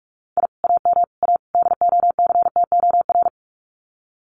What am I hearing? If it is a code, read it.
Morse code, heard as a keyed tone, about 46 wpm, 700 Hz. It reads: I AM A DOCTOR
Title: Common QSO Phrases Encore 45wpm